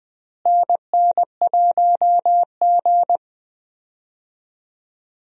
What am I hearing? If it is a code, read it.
Morse code, heard as a keyed tone, about 20 wpm, 700 Hz. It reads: NN1G
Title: USA Call Signs - 4 Characters - Repeated 3x - 20wpm